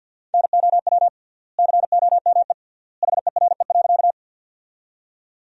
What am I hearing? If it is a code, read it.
Morse code, heard as a keyed tone, about 50 wpm, 700 Hz. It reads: NOW COME HERE.
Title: 3x Word-Spacing - Sentences from Top 200 Words - 50wpm